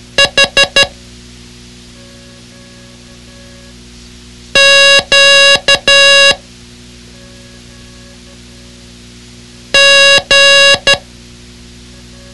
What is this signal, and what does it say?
Morse code, heard as a keyed tone, about 12 wpm, 2800 Hz. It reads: H Q G
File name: ham radio / CodeQuick / Code Quick 06